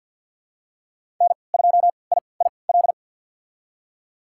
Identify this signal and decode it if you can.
Morse code, heard as a keyed tone, about 52 wpm, spaced as for 28 wpm, 700 Hz. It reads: N2IIL
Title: USA Call Signs - 5 Characters - 3x Character Spacing - 50wpm